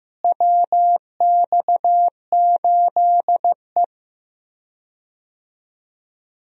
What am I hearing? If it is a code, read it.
Morse code, heard as a keyed tone, about 15 wpm, 700 Hz. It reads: WX8E
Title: USA Call Signs - 4 Characters - Repeated 2x - 15wpm